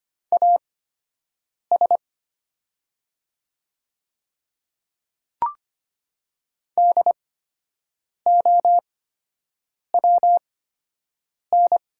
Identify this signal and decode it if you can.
Morse code, heard as a keyed tone, about 25 wpm, spaced as for 7 wpm, 700 Hz. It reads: AS DOWN
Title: 8x Character Spacing - Top 100 Words - 25wpm